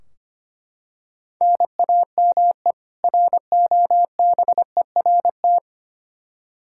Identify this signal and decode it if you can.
Morse code, heard as a keyed tone, about 25 wpm, 700 Hz. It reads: NAME ROBERT
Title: Practice QSO - Exchange 1 - Beginning - 25wpm